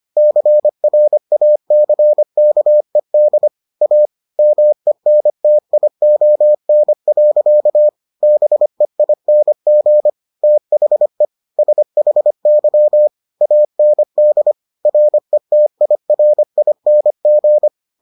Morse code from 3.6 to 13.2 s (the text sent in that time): A MENTION. BEING THE SHY